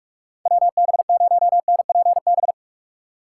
Morse code, heard as a keyed tone, about 46 wpm, 700 Hz. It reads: WB0NPB
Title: USA Call Signs - 6 Characters - 4x Character Spacing - 45wpm